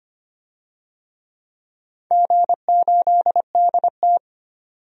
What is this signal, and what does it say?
Morse code, heard as a keyed tone, about 25 wpm, 700 Hz. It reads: G8DT
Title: DXCC-100 Call Signs - 4 Characters - 25wpm